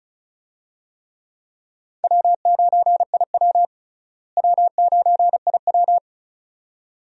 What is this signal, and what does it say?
Morse code, heard as a keyed tone, about 36 wpm, 700 Hz. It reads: W9IW W9IW
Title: USA Call Signs - 4 Characters - Repeated 2x - 35wpm